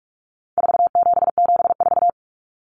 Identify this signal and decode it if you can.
Morse code, heard as a keyed tone, about 44 wpm, 700 Hz. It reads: 4774
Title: Common Cut Numbers - 4-Digit 45wpm